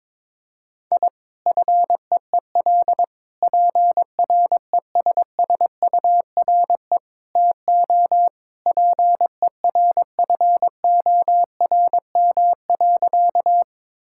Morse code, heard as a keyed tone, about 22 wpm, 700 Hz. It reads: I FEEL PRESSURE TO PERFORM.
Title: Sentences from Top 1000 Words - Review Part 6 - 22wpm